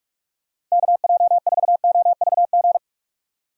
Morse code, heard as a keyed tone, about 46 wpm, 700 Hz. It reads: KJ4OVG
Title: USA Call Signs - 2x Character Spacing - 45wpm